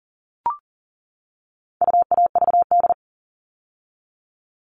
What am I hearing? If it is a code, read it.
Morse code, heard as a keyed tone, about 40 wpm, 700 Hz. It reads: UAVD
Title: All Cut Numbers - 4-Digit 40wpm